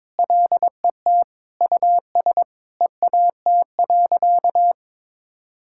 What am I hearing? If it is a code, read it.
Morse code, heard as a keyed tone, about 22 wpm, 700 Hz. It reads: LET US EAT.
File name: practice / 22wpm / Sentences with 3-Word Phrases - 22wpm